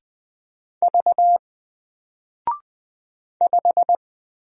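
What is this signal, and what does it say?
Morse code, heard as a keyed tone, about 20 wpm, 700 Hz. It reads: V 5
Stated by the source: Single letter and number Morse drill at 20 wpm, from N4LCD.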